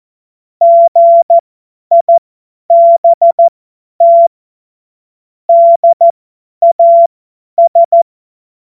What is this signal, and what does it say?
Morse code, heard as a keyed tone, about 14 wpm, spaced as for 10 wpm, 680 Hz. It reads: GIBT DAS
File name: Morsekurs / Witze / Witz 10. August 2021, 50 F70